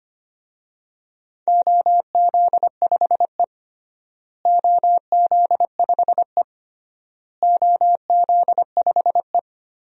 Morse code, heard as a keyed tone, about 25 wpm, 700 Hz. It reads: OZ5E OZ5E OZ5E
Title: DXCC-100 Call Signs - 4 Characters - Repeated 3x - 25wpm